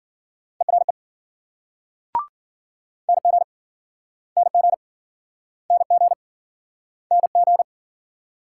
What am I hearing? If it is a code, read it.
Morse code, heard as a keyed tone, about 55 wpm, 700 Hz. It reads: ERE NG NG NG NG